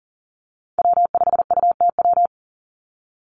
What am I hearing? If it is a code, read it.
Morse code, heard as a keyed tone, about 40 wpm, 700 Hz. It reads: W5UTW
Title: Call Signs Repeated 3x - 40wpm